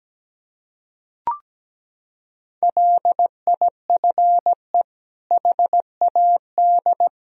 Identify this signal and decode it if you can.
Morse code, heard as a keyed tone, about 17 wpm, 700 Hz. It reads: LIFE HAD
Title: Sentences from Top 1000 Words - Review Part 4 - 17wpm